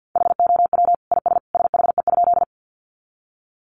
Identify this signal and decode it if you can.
Morse code, heard as a keyed tone, about 50 wpm, 700 Hz. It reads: HOW IS SHE?
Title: Sentences from Top 100 Words 50wpm